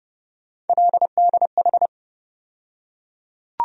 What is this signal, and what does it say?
Morse code, heard as a keyed tone, about 30 wpm, 700 Hz. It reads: LDH
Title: USA Call Sign Postfixes - 3 Letters 30wpm